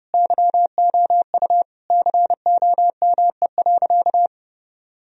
Morse code, heard as a keyed tone, about 30 wpm, 700 Hz. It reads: YOU COME.
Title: Sentences from Top 1000 Words - Review Part 4 - 30wpm